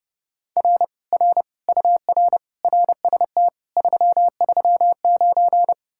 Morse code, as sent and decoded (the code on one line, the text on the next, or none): .-. / .-. / ..- .-. / .-. ... - / ...-- ...-- ----.
R R UR RST 339